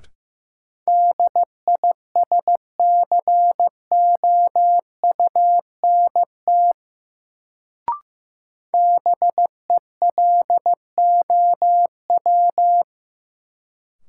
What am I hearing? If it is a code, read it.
Morse code, heard as a keyed tone, about 15 wpm, 700 Hz. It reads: DISCOUNT BELOW
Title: Sentences from Top 1200 Words - New Word Drill Alt - 15wpm